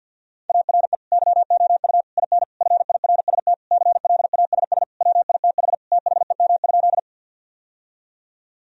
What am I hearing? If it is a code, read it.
Morse code, heard as a keyed tone, about 51 wpm, 700 Hz. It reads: ARE YOU IN FIRST CLASS WITH THEM?
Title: Sentences from Top 1000 Words - Review Part 2 - 50wpm